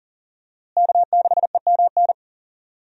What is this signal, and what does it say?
Morse code, heard as a keyed tone, about 40 wpm, 700 Hz. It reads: K6EMN
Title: USA Call Signs - 5 Characters - Repeated 2x - 40wpm